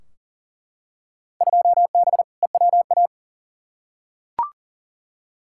Morse code, heard as a keyed tone, about 40 wpm, 700 Hz. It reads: 2B EWA
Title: Field Day Exchanges Encore - 40wpm